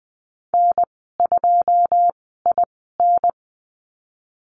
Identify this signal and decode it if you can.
Morse code, heard as a keyed tone, about 20 wpm, spaced as for 15 wpm, 700 Hz. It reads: N2IN
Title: 173 Lesson 28g - Number 9 - Call Signs 20wpm